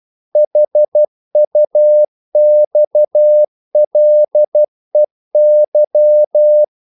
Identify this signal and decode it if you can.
Morse code, heard as a keyed tone, about 12 wpm, 600 Hz. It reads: HUXLEY
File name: CW / Practice / EnglishQuotations0112